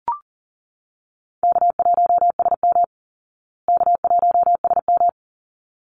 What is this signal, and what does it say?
Morse code, heard as a keyed tone, about 40 wpm, 700 Hz. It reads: K1SM K1SM
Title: Easy Call Signs Repeated 2x - 40wpm